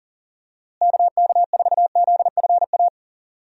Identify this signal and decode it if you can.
Morse code, heard as a keyed tone, about 40 wpm, 700 Hz. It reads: KK4ZFA
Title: USA Call Signs - 6 Characters - Repeated 3x - 40wpm